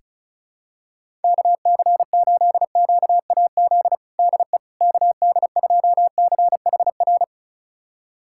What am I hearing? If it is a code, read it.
Morse code, heard as a keyed tone, about 36 wpm, 700 Hz. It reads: KC8QAZ DE KD2CHR
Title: USA Callsign DE Callsign - 6 Characters - 35wpm